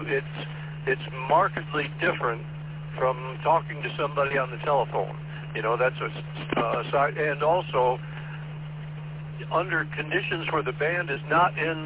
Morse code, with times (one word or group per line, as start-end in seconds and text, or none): none